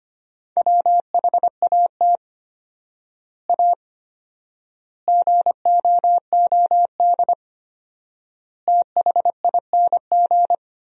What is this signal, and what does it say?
Morse code, heard as a keyed tone, about 25 wpm, 700 Hz. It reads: WHAT A GOOD THING
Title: 4x Word-Spacing - Sentences from Top 200 Words - 25wpm